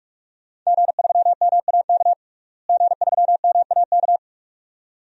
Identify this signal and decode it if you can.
Morse code, heard as a keyed tone, about 41 wpm, 700 Hz. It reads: G3MAK G3MAK
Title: DXCC-100 Call Signs - 5 Characters - Repeated 2x - 45wpm